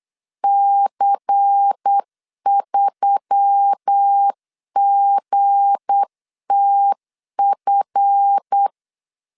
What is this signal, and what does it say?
Morse code, heard as a keyed tone, about 8 wpm, 800 Hz. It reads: C3GTF
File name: LEZIONI / 7 GF9Z / 7 GF 8wpm 3108